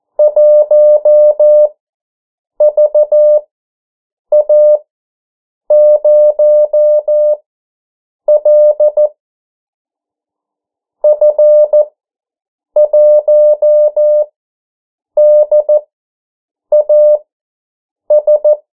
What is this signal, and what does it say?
Morse code, heard as a keyed tone, about 14 wpm, spaced as for 7 wpm, 600 Hz. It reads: 1VA0L F1DAS
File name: morse mp3 / 09 urok 8